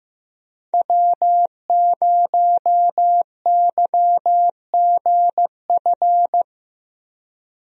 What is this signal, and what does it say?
Morse code, heard as a keyed tone, about 15 wpm, 700 Hz. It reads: W0YGF
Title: USA Call Signs - 5 Characters - Repeated 2x - 15wpm